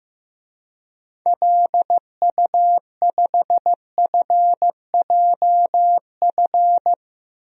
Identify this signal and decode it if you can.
Morse code, heard as a keyed tone, about 15 wpm, 700 Hz. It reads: LU5FJF
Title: DXCC-100 Call Signs - 6 Characters - 15wpm